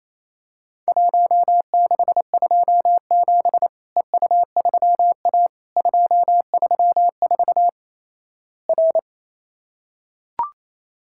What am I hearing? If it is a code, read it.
Morse code, heard as a keyed tone, about 28 wpm, 700 Hz. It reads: 1627 EU3A 234 R
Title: Worked All Europe DX Contest - QTC Exchange - 28wpm